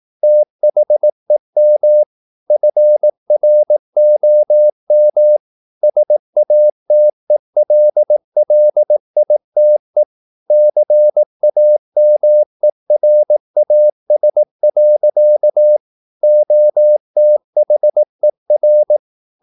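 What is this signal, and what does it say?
Morse code, heard as a keyed tone, about 18 wpm, 600 Hz. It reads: THEM FROM SATELLITE CAMERAS. OTHER
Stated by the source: Morse practice at 18 wpm, English text, from CW med Gnister.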